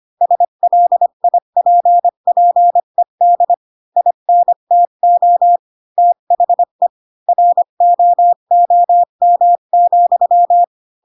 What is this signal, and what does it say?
Morse code, heard as a keyed tone, about 25 wpm, 700 Hz. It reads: SLIPPED INTO THE ROOM,
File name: Pinocchio / Pinocchio ch4 20 25 ik1hge